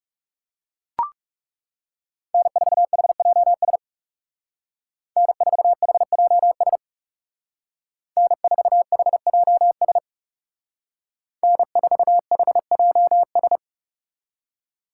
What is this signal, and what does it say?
Morse code, heard as a keyed tone, about 36 wpm, 700 Hz. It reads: N4HJS N4HJS N4HJS N4HJS
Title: Speed Racing - USA 1x3 Call Signs 30wpm